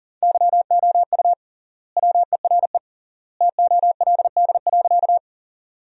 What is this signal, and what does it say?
Morse code, heard as a keyed tone, about 40 wpm, 700 Hz. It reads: YOU WERE TOLD.
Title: 3x Word-Spacing - Sentences from Top 400 Words - 40wpm